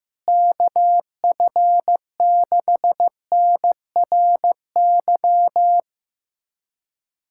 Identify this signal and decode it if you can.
Morse code, heard as a keyed tone, about 15 wpm, 700 Hz. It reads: KF6NRY
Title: USA Callsign DE Callsign 15wpm